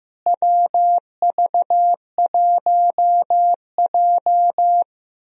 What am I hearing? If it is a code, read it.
Morse code, heard as a keyed tone, about 15 wpm, 700 Hz. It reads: WV1J
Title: USA Call Signs - 4 Characters - Repeated 2x - 15wpm